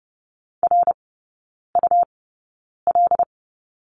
Morse code, heard as a keyed tone, about 30 wpm, spaced as for 9 wpm, 700 Hz. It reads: RUL
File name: Lesson 18e - Letter m - Three Characters 30wpm